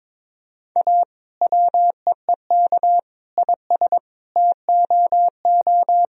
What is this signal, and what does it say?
Morse code, heard as a keyed tone, about 22 wpm, 700 Hz. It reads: A WEEK IS TOO
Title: Sentences from Top 1000 Words - Review Part 2 - 22wpm